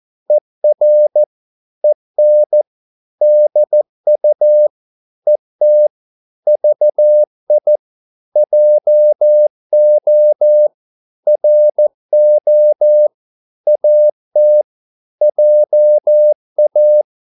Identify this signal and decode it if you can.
Morse code, heard as a keyed tone, about 14 wpm, 600 Hz. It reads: ER EN DU ET VI JO RO AT JA
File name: Korte ord DK 14wpm